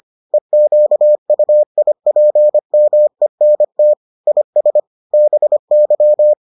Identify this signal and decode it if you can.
Morse code, heard as a keyed tone, about 25 wpm, 600 Hz. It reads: EQUIPMENT IS BY